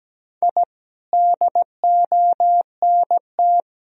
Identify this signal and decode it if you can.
Morse code, heard as a keyed tone, about 17 wpm, 700 Hz. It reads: I DONT
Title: Sentences from Top 1000 Words - Review Part 1 - 17wpm